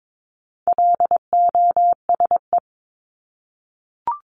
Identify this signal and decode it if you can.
Morse code, heard as a keyed tone, about 22 wpm, 700 Hz. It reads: LOSE
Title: Top 1001-1250 Words 22wpm